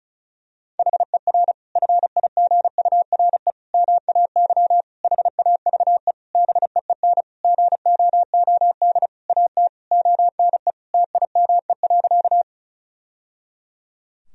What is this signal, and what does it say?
Morse code, heard as a keyed tone, about 34 wpm, 700 Hz. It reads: HER FIGURE MAY HAVE BEEN GOOD AT ONE TIME.
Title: Sentences from Top 1000 Words - Review Part 2 - 35wpm